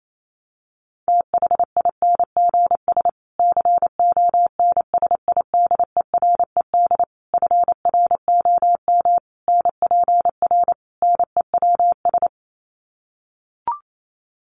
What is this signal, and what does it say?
Morse code, heard as a keyed tone, about 28 wpm, 700 Hz. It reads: THINGS CONSIDERED FROM NPR NEWS
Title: Sets of 5 Words Encore 28wpm